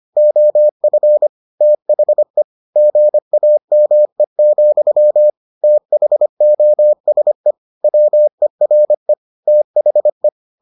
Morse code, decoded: OF THE GAME, THOSE WERE THE

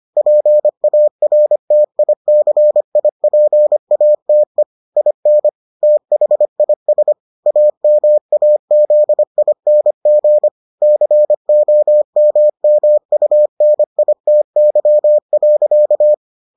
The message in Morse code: PARTICIPATE IN THIS AMAZING COMMUNITY.

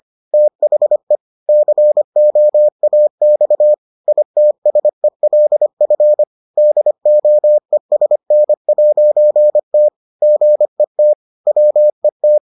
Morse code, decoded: THE COAX ITSELF DOESN'T GET WET

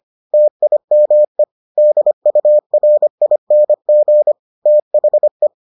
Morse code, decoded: TIME DURING THE

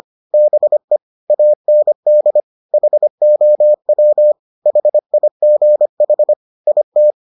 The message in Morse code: BE AND HOW HIGH IT

